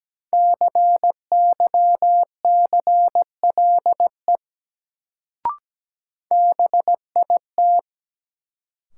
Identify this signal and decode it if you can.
Morse code, heard as a keyed tone, about 17 wpm, 700 Hz. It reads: CYCLE BIT
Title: Sentences from Top 900 Words - New Word Drill - 17wpm